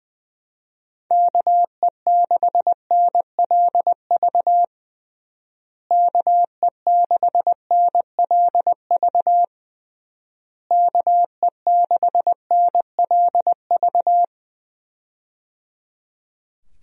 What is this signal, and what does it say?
Morse code, heard as a keyed tone, about 20 wpm, 700 Hz. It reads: KE6NLV KE6NLV KE6NLV
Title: USA Call Signs - 6 Characters - Repeated 3x - 20wpm